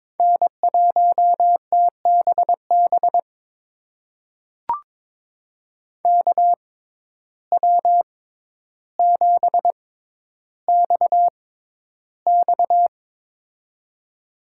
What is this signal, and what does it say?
Morse code, heard as a keyed tone, about 22 wpm, spaced as for 11 wpm, 700 Hz. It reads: N1TBB KW7XX
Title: USA Call Signs - 5 Characters - 6x Character Spacing - 22wpm